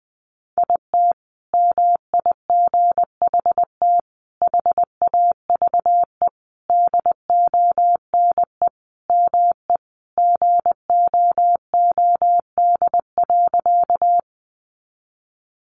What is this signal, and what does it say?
Morse code, heard as a keyed tone, about 20 wpm, 700 Hz. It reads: IT MIGHT HAVE DONE ME GOOD.
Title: Sentences from Top 500 Words 20wpm